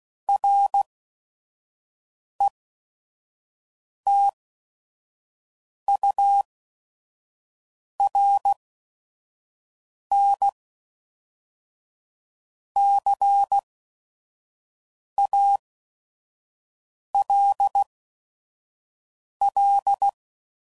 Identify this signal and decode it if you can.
Morse code, heard as a keyed tone, about 16 wpm, spaced as for 5 wpm, 800 Hz. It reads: RETURN CALL
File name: CW Code Groups @5WPM / QSOs / kc7sac